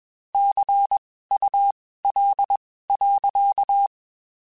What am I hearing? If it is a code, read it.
Morse code, heard as a keyed tone, about 21 wpm, spaced as for 15 wpm, 800 Hz. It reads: CUL.
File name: (spaced) / QSOs Spaced-mp3s / kc7haa @20 sp